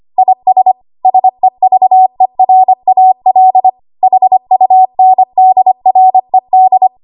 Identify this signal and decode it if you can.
Morse code, heard as a keyed tone, about 25 wpm, 750 Hz. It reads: IS SEVERAL HUNDRED
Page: W1AW Web Code Practice Files